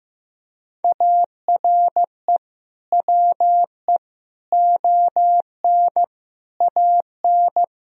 Morse code, heard as a keyed tone, about 16 wpm, 700 Hz. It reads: ARE WE ON AN
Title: Sentences from Top 1000 Words - Review Part 3 - 15wpm